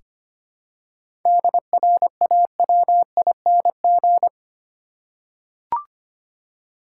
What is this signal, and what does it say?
Morse code, heard as a keyed tone, about 25 wpm, 700 Hz. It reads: DRAWING
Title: Seven Letter Words - 25wpm